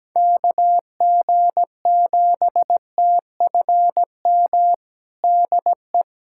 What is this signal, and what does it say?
Morse code, heard as a keyed tone, about 17 wpm, 700 Hz. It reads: KG7TFM DE
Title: USA Callsign DE Callsign - 6 Characters - 17wpm